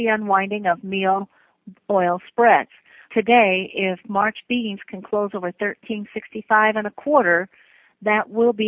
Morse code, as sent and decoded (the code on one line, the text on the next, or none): none